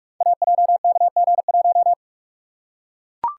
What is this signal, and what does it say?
Morse code, heard as a keyed tone, about 44 wpm, 700 Hz. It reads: AJKG1
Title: Code Groups - 5 Letter:Numbers - 45wpm